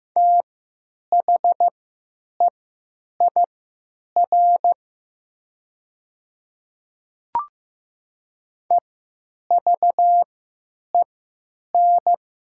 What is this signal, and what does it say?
Morse code, heard as a keyed tone, about 15 wpm, spaced as for 8 wpm, 700 Hz. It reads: THEIR EVEN
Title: 3x Character Spacing - Top 200 Words - 15wpm